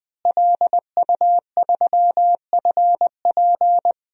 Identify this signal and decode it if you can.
Morse code, heard as a keyed tone, about 20 wpm, 700 Hz. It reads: LU3FP
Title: DXCC-100 Call Signs - 5 Characters - Repeated 2x - 20wpm